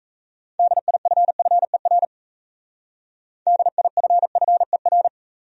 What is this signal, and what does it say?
Morse code, heard as a keyed tone, about 34 wpm, 700 Hz. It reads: DIFFER DIFFER
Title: Speed-Racing Top 300 Words 28wpm